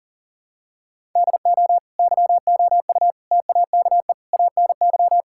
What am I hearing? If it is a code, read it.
Morse code, heard as a keyed tone, about 40 wpm, 700 Hz. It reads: DO YOU TAKE ANY
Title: Sentences from Top 1000 Words - Review Part 6 - 40wpm